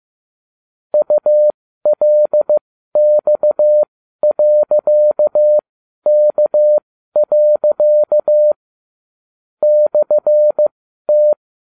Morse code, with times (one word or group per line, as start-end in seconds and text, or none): 0.9-8.5 ULX.K.
9.6-11.3 /T